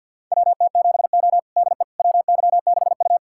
Morse code, heard as a keyed tone, about 47 wpm, 690 Hz. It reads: WT7O DE WY6U
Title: USA Callsign DE Callsign - 4 Characters - 50wpm